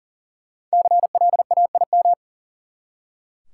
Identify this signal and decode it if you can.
Morse code, heard as a keyed tone, about 40 wpm, 700 Hz. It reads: CLAIM
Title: Five Letter Words - 40wpm